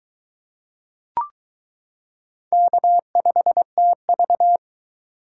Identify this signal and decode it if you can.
Morse code, heard as a keyed tone, about 22 wpm, 700 Hz. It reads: K5TV